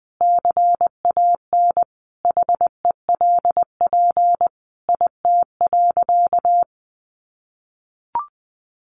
Morse code, heard as a keyed tone, about 19 wpm, 700 Hz. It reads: CAN HELP IT.
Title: Sentences from Top 1100 Words 20wpm